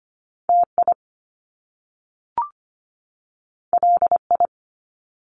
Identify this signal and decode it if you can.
Morse code, heard as a keyed tone, about 25 wpm, 700 Hz. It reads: TI LI
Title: Top Two Letter Combinations in Words 25wpm